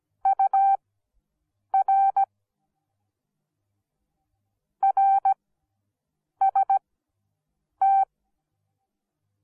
Morse code, heard as a keyed tone, about 17 wpm, spaced as for 7 wpm, 780 Hz. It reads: UR RST